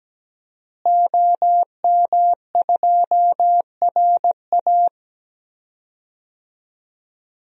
Morse code, heard as a keyed tone, about 17 wpm, 700 Hz. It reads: OM2RA
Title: DXCC-100 Call Signs - 5 Characters - Repeated 2x - 17wpm